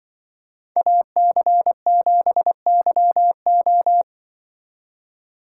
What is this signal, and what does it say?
Morse code, heard as a keyed tone, about 24 wpm, 700 Hz. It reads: AC7YO
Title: Speed Racing - USA 2x2 Call Signs 20wpm